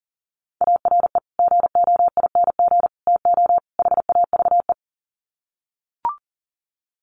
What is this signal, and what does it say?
Morse code, heard as a keyed tone, about 40 wpm, 700 Hz. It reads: ARE GOING TO HAVE
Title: Sets of 4 Words 40wpm